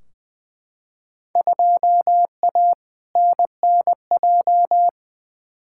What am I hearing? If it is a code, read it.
Morse code, heard as a keyed tone, about 21 wpm, 700 Hz. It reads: 2A NNJ
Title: Field Day Exchanges - 20wpm